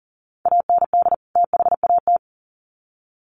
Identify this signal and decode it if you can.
Morse code, heard as a keyed tone, about 40 wpm, 700 Hz. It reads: AND THAT
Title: Sets of 2 Words 40wpm